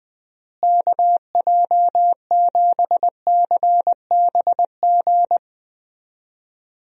Morse code, heard as a keyed tone, about 20 wpm, 700 Hz. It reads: KJ7CBG
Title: USA Call Signs - 6 Characters - Repeated 2x - 20wpm